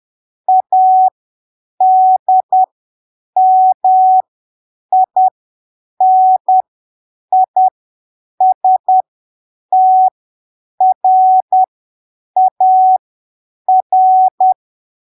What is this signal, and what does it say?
Morse code, heard as a keyed tone, about 10 wpm, spaced as for 7 wpm, 750 Hz. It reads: ADMINISTRAR
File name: arquivos / texto / texto 06 10ppm